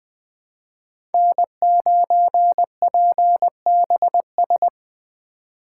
Morse code, heard as a keyed tone, about 20 wpm, 700 Hz. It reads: N9PBS
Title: USA Call Signs - 5 Characters - 4x Character Spacing - 20wpm